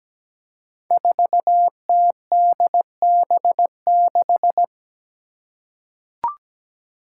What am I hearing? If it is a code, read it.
Morse code, heard as a keyed tone, about 17 wpm, 700 Hz. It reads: 4TDB6
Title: Five Challenging Letter-Number Combinations 17wpm